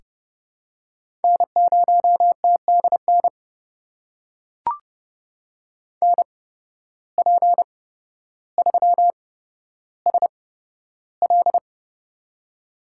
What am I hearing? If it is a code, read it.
Morse code, heard as a keyed tone, about 30 wpm, spaced as for 13 wpm, 700 Hz. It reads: N0TDN NP3SL
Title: USA Call Signs - 5 Characters - 8x Character Spacing - 30wpm